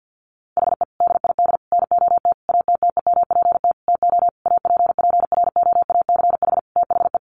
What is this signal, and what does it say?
Morse code, heard as a keyed tone, about 50 wpm, 700 Hz. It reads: HE DID NOT ATTEMPT TO APPROACH THE
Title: Sentences from Top 1400 Words 50wpm